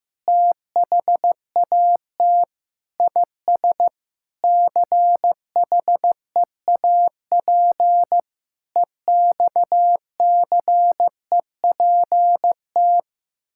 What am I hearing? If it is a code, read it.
Morse code, heard as a keyed tone, about 16 wpm, 700 Hz. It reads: THAT IS CHEAP EXCEPT
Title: Sentences from Top 1600 Words 15wpm